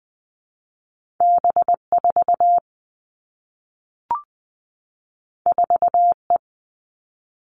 Morse code, heard as a keyed tone, about 20 wpm, 700 Hz. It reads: B 4 4 E
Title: Two Challenging Letter-Number Combinations 20wpm